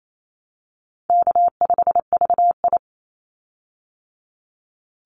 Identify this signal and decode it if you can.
Morse code, heard as a keyed tone, about 28 wpm, 700 Hz. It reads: K5VI
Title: Call Signs Repeated 2x - 28wpm